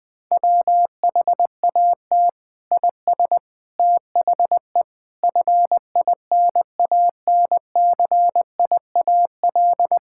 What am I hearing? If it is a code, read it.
Morse code, heard as a keyed tone, about 21 wpm, 700 Hz. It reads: WHAT IS THE FINANCIAL
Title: Sentences from Top 1000 Words - Review Part 5 - 20wpm